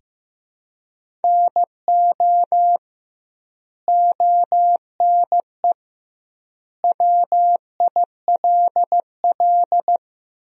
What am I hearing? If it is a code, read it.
Morse code, heard as a keyed tone, about 16 wpm, 700 Hz. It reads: NO ONE WILL
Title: 2x Word-Spacing - Sentences from Top 100 Words - 15wpm